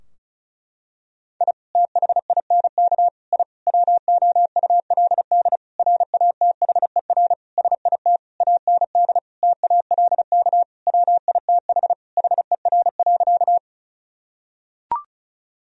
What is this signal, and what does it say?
Morse code, heard as a keyed tone, about 34 wpm, 700 Hz. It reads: I THINK I WOULD RATHER SIT AND TALK WITH HER.
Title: Sentences from Top 1000 Words - Review Part 4 - 35wpm